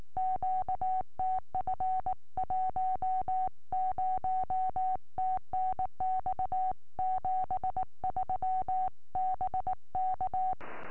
Morse code, heard as a keyed tone, about 19 wpm, 750 Hz. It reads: QTF 10 TNX 73 BK